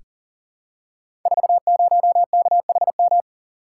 Morse code, heard as a keyed tone, about 40 wpm, 700 Hz. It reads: 40KHM
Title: Code Groups - 5 Letter:Numbers - 40wpm